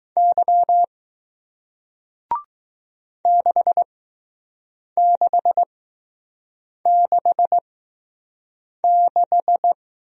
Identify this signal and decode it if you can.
Morse code, heard as a keyed tone, about 20 wpm, spaced as for 6 wpm, 700 Hz. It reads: Y 6666